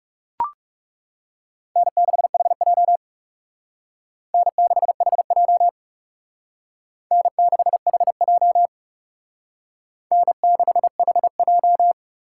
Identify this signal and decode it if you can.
Morse code, heard as a keyed tone, about 37 wpm, 700 Hz. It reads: N6HJ N6HJ N6HJ N6HJ
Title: Speed Racing - USA 1x2 Call Signs 30wpm